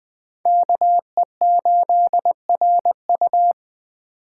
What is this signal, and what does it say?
Morse code, heard as a keyed tone, about 20 wpm, 700 Hz. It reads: KE8RU
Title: K1USN SST QSO Practice - 20wpm